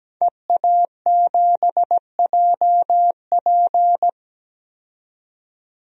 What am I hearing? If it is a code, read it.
Morse code, heard as a keyed tone, about 17 wpm, 700 Hz. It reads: EA7JP
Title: DXCC-100 Call Signs - 5 Characters - 17wpm